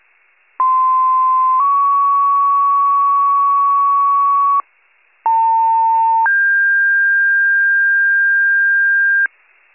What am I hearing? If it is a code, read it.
Two-tone page, heard as a one-second tone, then a three-second tone, about 1040 then 1120 Hz, about 900 then 1600 Hz. It is heard from 0.6 to 9.3 s.